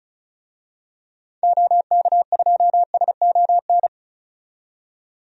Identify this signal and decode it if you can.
Morse code, heard as a keyed tone, about 36 wpm, 700 Hz. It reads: OK2SON
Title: DXCC-100 Call Signs - 6 Characters - 35wpm